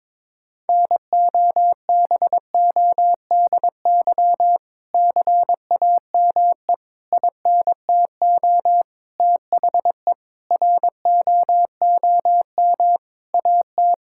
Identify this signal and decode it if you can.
Morse code, heard as a keyed tone, about 22 wpm, 700 Hz. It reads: NOBODY CAME INTO THE ROOM AT
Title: Sentences from Top 1600 Words 22wpm